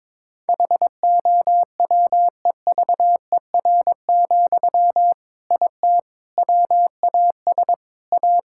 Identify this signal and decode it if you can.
Morse code, heard as a keyed tone, about 22 wpm, 700 Hz. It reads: HOWEVER, IT WAS A
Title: Sentences from Top 1000 Words - Review Part 6 - 22wpm